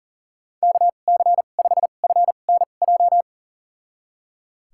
Morse code, heard as a keyed tone, about 37 wpm, spaced as for 28 wpm, 700 Hz. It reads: KC5FNJ
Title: USA Call Signs - 6 Characters - 2x Character Spacing - 40wpm